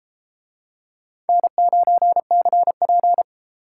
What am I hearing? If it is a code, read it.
Morse code, heard as a keyed tone, about 31 wpm, 700 Hz. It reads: N9CP